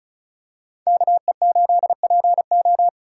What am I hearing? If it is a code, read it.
Morse code, heard as a keyed tone, about 34 wpm, 700 Hz. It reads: KE8PO
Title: USA Callsign DE Callsign - 5 Characters - 35wpm